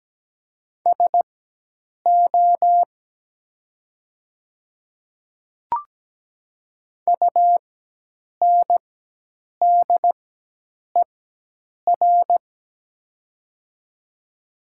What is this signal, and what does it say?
Morse code, heard as a keyed tone, about 17 wpm, spaced as for 8 wpm, 700 Hz. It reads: SO UNDER
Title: 4x Character Spacing - Top 300 Words - 17wpm